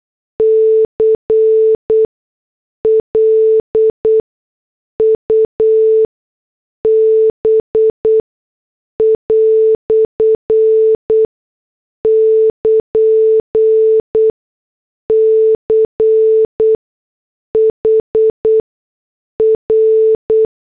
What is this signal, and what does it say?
Morse code, heard as a keyed tone, about 8 wpm, spaced as for 6 wpm, 440 Hz. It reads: CLUB" <KN> CHR
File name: morse / headlines